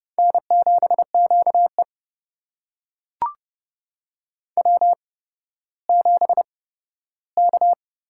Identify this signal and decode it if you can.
Morse code, heard as a keyed tone, about 30 wpm, 700 Hz. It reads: N 7 Q E W 7 K
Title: USA Call Signs - 4 Characters - 8x Character Spacing - 30wpm